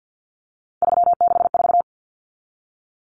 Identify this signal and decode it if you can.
Morse code, heard as a keyed tone, about 50 wpm, 700 Hz. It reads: 364
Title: Common Cut Numbers - 3-Digit 50wpm